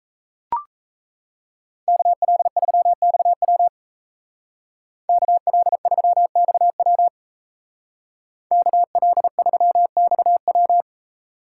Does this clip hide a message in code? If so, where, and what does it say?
Yes: Morse code from 1.9 to 10.8 s, heard as a keyed tone, about 37 wpm, 700 Hz. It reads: KL3XW KL3XW KL3XW